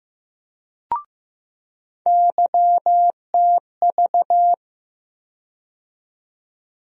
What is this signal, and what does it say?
Morse code, heard as a keyed tone, about 15 wpm, 700 Hz. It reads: YTV
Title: USA Call Sign Postfixes - 3 Letters 15wpm